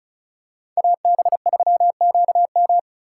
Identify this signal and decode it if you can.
Morse code, heard as a keyed tone, about 36 wpm, 700 Hz. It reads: AB3QM
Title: USA Call Signs - 5 Characters - 3x Character Spacing - 35wpm